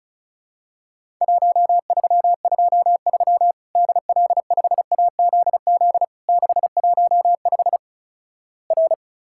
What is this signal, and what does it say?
Morse code, heard as a keyed tone, about 36 wpm, 700 Hz. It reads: 1323 DL5AZZ 615 R
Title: Worked All Europe DX Contest - QTC Exchange - 35wpm